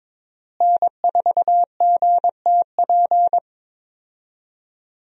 Morse code, heard as a keyed tone, about 22 wpm, 700 Hz. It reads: N4GTP
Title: USA Call Signs - 5 Characters - Repeated 2x - 22wpm